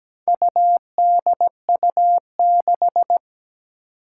Morse code, heard as a keyed tone, about 17 wpm, 700 Hz. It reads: UDU6
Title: Four Challenging Letter-Number Combinations 17wpm